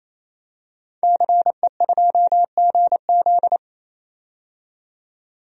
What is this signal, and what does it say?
Morse code, heard as a keyed tone, about 28 wpm, 700 Hz. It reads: CE2GZ
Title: DXCC-100 Call Signs - 5 Characters - 28wpm